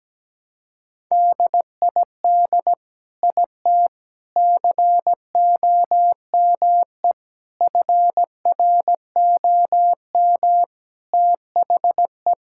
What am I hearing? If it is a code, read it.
Morse code, heard as a keyed tone, about 17 wpm, 700 Hz. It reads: DID IT COME FROM THE
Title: Sentences from Top 1000 Words - Review Part 5 - 17wpm